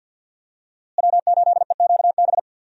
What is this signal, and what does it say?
Morse code, heard as a keyed tone, about 52 wpm, 700 Hz. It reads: W8EQB
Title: USA Call Signs - 5 Characters - Repeated 2x - 50wpm